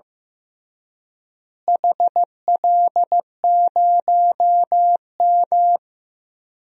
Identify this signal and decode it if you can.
Morse code, heard as a keyed tone, about 15 wpm, 700 Hz. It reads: HL0M
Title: DXCC-100 Call Signs - 4 Characters - Repeated 3x - 15wpm